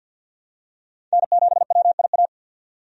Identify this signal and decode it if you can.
Morse code, heard as a keyed tone, about 53 wpm, 700 Hz. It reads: N7WIA
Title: USA Call Signs - 5 Characters - 50wpm